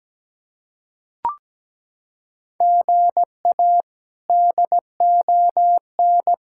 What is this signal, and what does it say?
Morse code, heard as a keyed tone, about 17 wpm, 700 Hz. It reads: GA DON
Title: K1USN SST QSO Practice - 17wpm